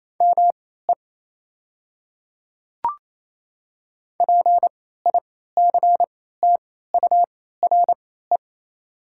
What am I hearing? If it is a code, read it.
Morse code, heard as a keyed tone, about 28 wpm, spaced as for 16 wpm, 700 Hz. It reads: ME PICTURE
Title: 3x Character Spacing - Top 200 Words - 28wpm